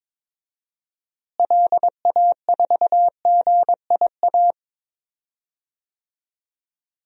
Morse code, heard as a keyed tone, about 22 wpm, 700 Hz. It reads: LA4GIA
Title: DXCC-100 Call Signs - 6 Characters - Repeated 2x - 22wpm